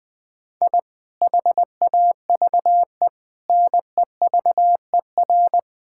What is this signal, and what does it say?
Morse code, heard as a keyed tone, about 20 wpm, 700 Hz. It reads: I HAVE NEVER
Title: Sentences from Top 1000 Words - Review Part 5 - 20wpm